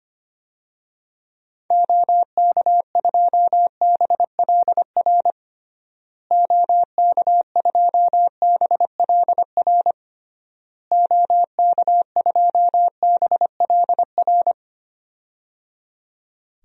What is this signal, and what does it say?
Morse code, heard as a keyed tone, about 25 wpm, 700 Hz. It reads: OK2BLR OK2BLR OK2BLR
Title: DXCC-100 Call Signs - 6 Characters - Repeated 3x - 25wpm